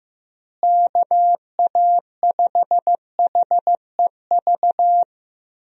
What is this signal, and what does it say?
Morse code, heard as a keyed tone, about 16 wpm, 700 Hz. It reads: KA5HEV
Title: USA Call Signs - 6 Characters - Repeated 3x - 15wpm